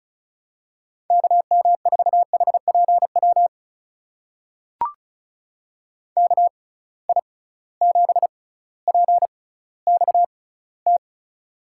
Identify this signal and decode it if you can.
Morse code, heard as a keyed tone, about 36 wpm, spaced as for 18 wpm, 700 Hz. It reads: KM4HPW KI7PXT
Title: USA Call Signs - 6 Characters - 6x Character Spacing - 35wpm